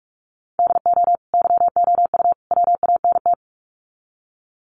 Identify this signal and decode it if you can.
Morse code, heard as a keyed tone, about 44 wpm, 700 Hz. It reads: DO YOU WANT
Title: Sets of 3 Words 45wpm